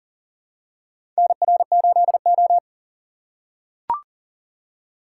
Morse code, heard as a keyed tone, about 43 wpm, 700 Hz. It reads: NR8O
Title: USA Call Signs - 4 Characters - 6x Character Spacing - 40wpm